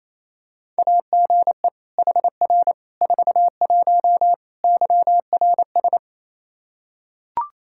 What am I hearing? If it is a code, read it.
Morse code, heard as a keyed tone, about 28 wpm, 700 Hz. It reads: AGE HR 41 YRS
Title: Practice QSO - Exchange 3 - More Info - 28wpm